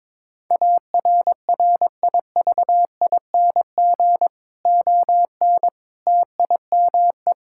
Morse code, heard as a keyed tone, about 22 wpm, 700 Hz. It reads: ARRIVING ON TIME
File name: Difficult Sentences with Hints - 22wpm